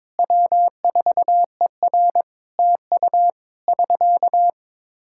Morse code, heard as a keyed tone, about 22 wpm, 700 Hz. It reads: W4ER TU <SK>
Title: Practice QSO - Exchange 4 - Ending - 22wpm